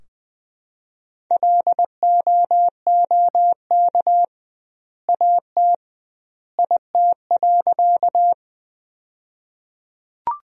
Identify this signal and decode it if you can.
Morse code, heard as a keyed tone, about 21 wpm, 700 Hz. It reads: LOOK AT IT.
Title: 2x Word-Spacing - Sentences from Top 100 Words - 20wpm